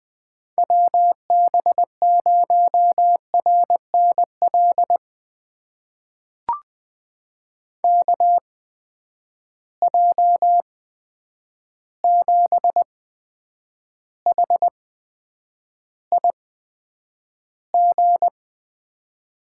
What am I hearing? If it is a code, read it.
Morse code, heard as a keyed tone, about 20 wpm, spaced as for 9 wpm, 700 Hz. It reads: WB0RNL KJ7HIG
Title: USA Call Signs - 6 Characters - 8x Character Spacing - 20wpm